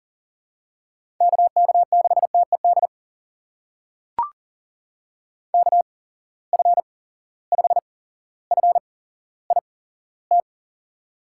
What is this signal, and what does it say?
Morse code, heard as a keyed tone, about 39 wpm, spaced as for 17 wpm, 700 Hz. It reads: KK6TED KF5FIT
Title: USA Call Signs - 6 Characters - 8x Character Spacing - 40wpm